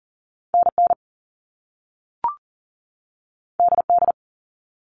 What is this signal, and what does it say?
Morse code, heard as a keyed tone, about 38 wpm, 700 Hz. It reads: NN DD
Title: All Cut Numbers - 2-Digit 40wpm